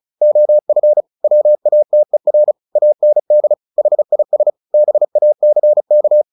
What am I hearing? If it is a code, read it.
Morse code, heard as a keyed tone, about 34 wpm, 600 Hz. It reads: OF WATER AND HIS BACK
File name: the old man and the sea 2 35wpm